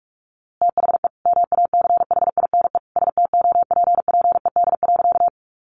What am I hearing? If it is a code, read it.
Morse code, heard as a keyed tone, about 44 wpm, 700 Hz. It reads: THE MACHINE STOPPED.
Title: A Sound Of Thunder 45wpm